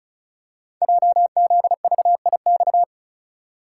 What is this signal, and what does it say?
Morse code, heard as a keyed tone, about 36 wpm, 690 Hz. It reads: JZVIX
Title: Code Groups - 5 Letters - 35wpm